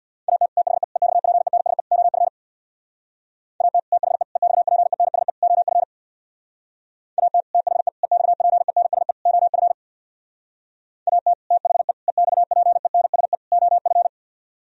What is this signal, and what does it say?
Morse code, heard as a keyed tone, about 64 wpm, 700 Hz. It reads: AT THE EXPENSE OF AT THE EXPENSE OF AT THE EXPENSE OF AT THE EXPENSE OF